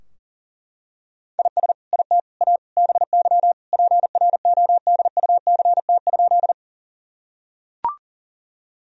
Morse code, heard as a keyed tone, about 37 wpm, 700 Hz. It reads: IS IT A BY PRODUCT?
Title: Sentences from Top 1000 Words - Review Part 2 - 40wpm